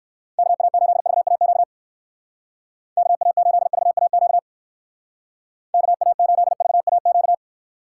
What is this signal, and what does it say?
Morse code, heard as a keyed tone, about 59 wpm, 700 Hz. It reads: KA8VAQ KA8VAQ KA8VAQ